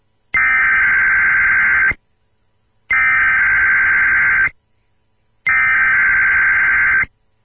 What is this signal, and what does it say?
Emergency Alert System (SAME) tones (header bursts), heard as harsh buzzing bursts. It is heard from 0.4 to 7.0 s.